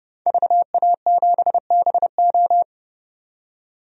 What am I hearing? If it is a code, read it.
Morse code, heard as a keyed tone, about 29 wpm, 690 Hz. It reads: VA7BO
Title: DXCC-100 Call Signs - 5 Characters - 30wpm